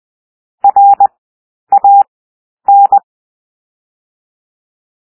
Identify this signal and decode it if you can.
Morse code, heard as a keyed tone, about 20 wpm, 800 Hz. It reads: R A N
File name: cwmp3 / W1 SOUND WORDS 20 wpm